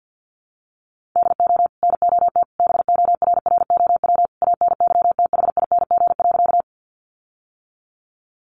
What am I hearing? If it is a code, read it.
Morse code, heard as a keyed tone, about 50 wpm, 700 Hz. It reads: DO NOT BORROW ANYTHING.
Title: Sentences from Top 1500 Words 50wpm